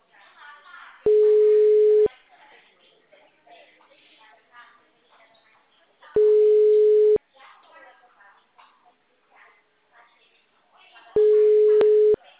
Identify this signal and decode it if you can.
Ringback tone (Europe, 425 Hz), with speech.
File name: out- -2007- - - .180